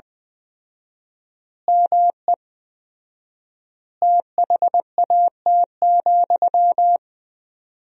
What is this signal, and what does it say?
Morse code, heard as a keyed tone, about 21 wpm, 700 Hz. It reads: ME THAT,
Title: 4x Word-Spacing - Sentences from Top 500 Words - 20wpm